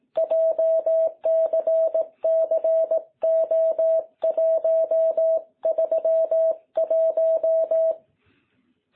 Morse code, heard as a keyed tone, about 18 wpm, 660 Hz. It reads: JCCO131
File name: jh8cbh / 2021cw